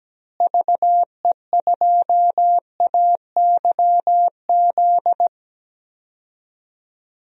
Morse code, heard as a keyed tone, about 17 wpm, 700 Hz. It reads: VE2AYZ
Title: DXCC-100 Call Signs - 6 Characters - Repeated 3x - 17wpm